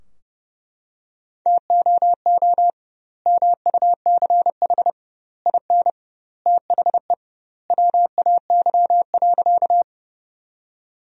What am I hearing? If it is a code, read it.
Morse code, heard as a keyed tone, about 30 wpm, 700 Hz. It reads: TOO MUCH IN THE WAY.
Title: 2x Word-Spacing - Sentences from Top 200 Words - 30wpm